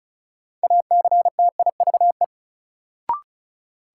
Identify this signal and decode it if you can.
Morse code, heard as a keyed tone, about 34 wpm, 700 Hz. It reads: ACTIVE
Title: Sentences from Top 900 Words - New Word Drill - 35wpm